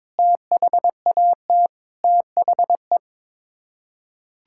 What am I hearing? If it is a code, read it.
Morse code, heard as a keyed tone, about 22 wpm, 700 Hz. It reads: THAT THE
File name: practice / 22wpm / Sets of 8 Words - 22wpm